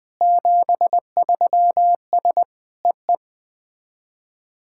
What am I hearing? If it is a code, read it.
Morse code, heard as a keyed tone, about 21 wpm, 700 Hz. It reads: 73S EE
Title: SOTA QSO Practice 20wpm